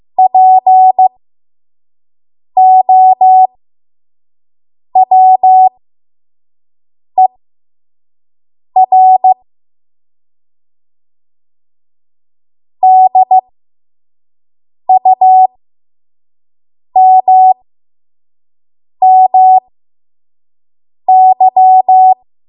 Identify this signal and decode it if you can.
Morse code, heard as a keyed tone, about 15 wpm, spaced as for 5 wpm, 750 Hz. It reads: POWER DUMMY
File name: Morse / 05WPM